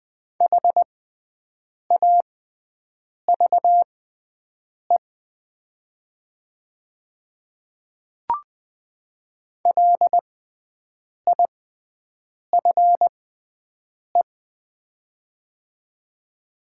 Morse code, heard as a keyed tone, about 20 wpm, spaced as for 7 wpm, 700 Hz. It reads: HAVE LIFE
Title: 6x Character Spacing - Top 500 Words - 20wpm